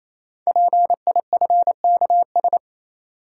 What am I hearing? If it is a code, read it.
Morse code, heard as a keyed tone, about 29 wpm, 700 Hz. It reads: PIFKS
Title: Code Groups - 5 Letters - 28wpm